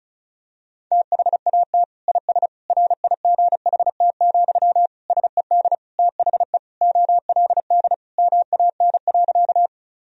Morse code, heard as a keyed tone, about 36 wpm, 700 Hz. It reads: THAT IS RIGHT, SED THE OLD MAN.
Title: Sentences from Top 200 Words and Abbreviations 35wpm